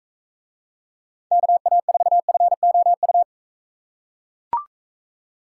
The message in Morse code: KA4FOU

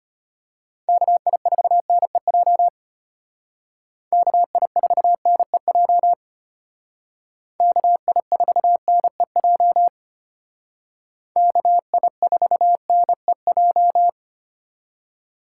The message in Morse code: KI4NEJ KI4NEJ KI4NEJ KI4NEJ

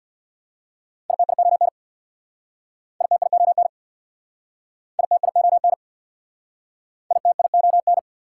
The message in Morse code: ITION ITION ITION ITION

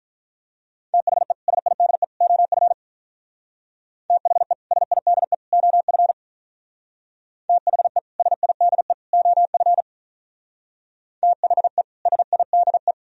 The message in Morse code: THE SIDE OF THE SIDE OF THE SIDE OF THE SIDE

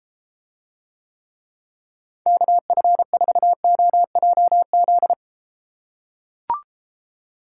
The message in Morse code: KF4OJZ

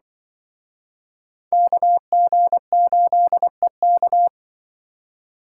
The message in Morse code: KG8EK

